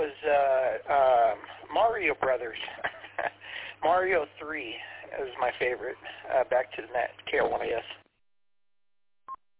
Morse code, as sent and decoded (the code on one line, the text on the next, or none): none